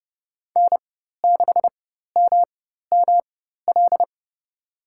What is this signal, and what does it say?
Morse code, heard as a keyed tone, about 30 wpm, spaced as for 14 wpm, 700 Hz. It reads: N6MML
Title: USA Call Signs - 5 Characters - 4x Character Spacing - 30wpm